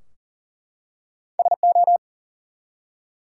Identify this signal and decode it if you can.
Morse code, heard as a keyed tone, about 40 wpm, 700 Hz. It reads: SO
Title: Two Letter Words - 40wpm